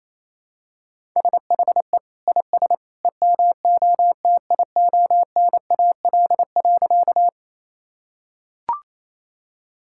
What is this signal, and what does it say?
Morse code, heard as a keyed tone, about 28 wpm, 700 Hz. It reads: SHE IS EMOTIONAL.
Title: Sentences from Top 1000 Words - Review Part 5 - 28wpm